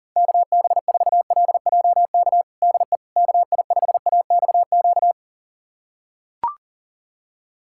Morse code, heard as a keyed tone, about 40 wpm, 700 Hz. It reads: KB4LJK DE KI5AXQ
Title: USA Callsign DE Callsign - 6 Characters - 40wpm